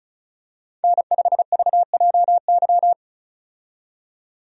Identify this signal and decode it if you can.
Morse code, heard as a keyed tone, about 36 wpm, 700 Hz. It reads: N5VJY
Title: USA Call Signs - 5 Characters - 35wpm